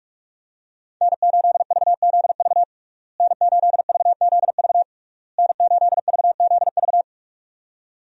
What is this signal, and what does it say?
Morse code, heard as a keyed tone, about 47 wpm, 700 Hz. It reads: N8VZV N8VZV N8VZV
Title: USA Call Signs - 5 Characters - Repeated 3x - 45wpm